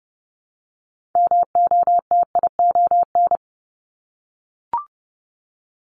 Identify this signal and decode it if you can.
Morse code, heard as a keyed tone, about 30 wpm, 700 Hz. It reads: MOTION
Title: Multi-letter Phonemes Repeated 3x 30wpm